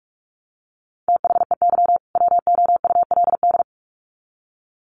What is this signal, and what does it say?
Morse code, heard as a keyed tone, about 44 wpm, 700 Hz. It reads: THEY WOULD
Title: Sets of 2 Words 45wpm